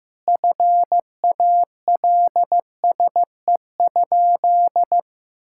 Morse code, heard as a keyed tone, about 14 wpm, 700 Hz. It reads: FALSE?
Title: Sentences from Top 1600 Words 15wpm